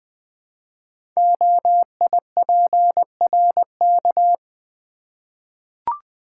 Morse code, heard as a keyed tone, about 20 wpm, 700 Hz. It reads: OIPRK
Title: Code Groups - 5 Letters - 20wpm